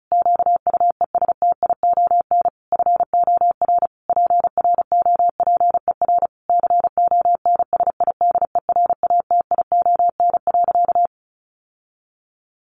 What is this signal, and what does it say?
Morse code, heard as a keyed tone, about 34 wpm, 700 Hz. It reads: QUESTION FOR PROPER CONSIDERATION.
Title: Sentences from Top 1500 Words 35wpm